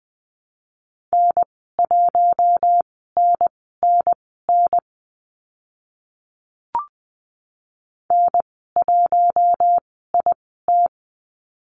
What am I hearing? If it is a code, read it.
Morse code, heard as a keyed tone, about 20 wpm, spaced as for 14 wpm, 700 Hz. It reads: N1NNN N1IT
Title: 042 Lesson 8g - Number 1 - Call Signs 20wpm